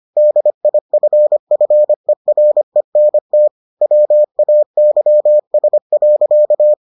Morse code, heard as a keyed tone, about 25 wpm, 600 Hz. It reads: DIFFERENT WAYS.